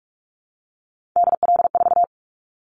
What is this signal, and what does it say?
Morse code, heard as a keyed tone, about 44 wpm, 700 Hz. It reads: DL4
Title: Callsign Prefixes for Top 25 DXCC - Repeated 3x - 45wpm